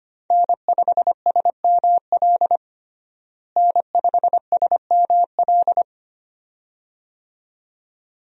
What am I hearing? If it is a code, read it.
Morse code, heard as a keyed tone, about 25 wpm, 700 Hz. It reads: N5SML N5SML
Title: USA Call Signs - 5 Characters - Repeated 2x - 25wpm